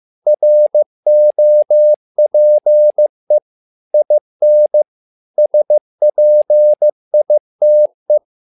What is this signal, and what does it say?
Morse code, heard as a keyed tone, about 15 wpm, 600 Hz. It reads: ROPE IN SPITE
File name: learning center / tower trouble / tower trouble 15wpm